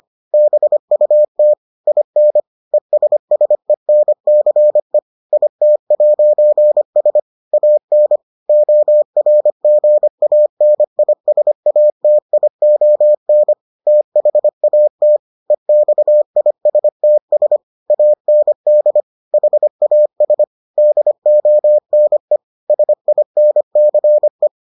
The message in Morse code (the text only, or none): BUT IN ESSENCE IT'S AN ORGANISATION THAT EXISTS AND HAS DONE SINCE